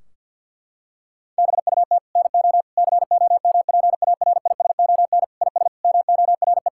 Morse code, heard as a keyed tone, about 51 wpm, 700 Hz. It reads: BUT NO COMPARISON IS MORE
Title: Sentences from Top 1600 Words 50wpm